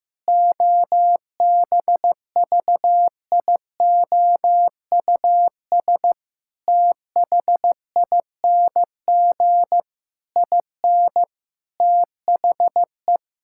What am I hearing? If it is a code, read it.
Morse code, heard as a keyed tone, about 15 wpm, 700 Hz. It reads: OBVIOUS THING IN THE
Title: Sentences from Top 1600 Words 15wpm